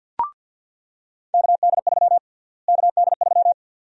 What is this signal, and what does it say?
Morse code, heard as a keyed tone, about 48 wpm, 700 Hz. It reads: KD3 KD3
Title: USA Call Sign Prefixes - Repeated 2x - 50wpm